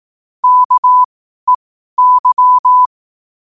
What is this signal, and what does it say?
Morse code, heard as a keyed tone, about 18 wpm, spaced as for 12 wpm, 1000 Hz. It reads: KEY
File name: qotdfiles / 12wpm